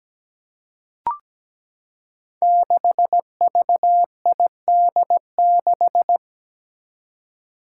Morse code, heard as a keyed tone, about 17 wpm, 700 Hz. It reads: 6VID6
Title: Five Challenging Letter-Number Combinations 17wpm